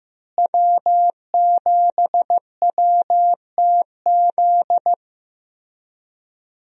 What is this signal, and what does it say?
Morse code, heard as a keyed tone, about 15 wpm, 700 Hz. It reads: W7WTZ
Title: USA Call Signs - 5 Characters - Repeated 3x - 15wpm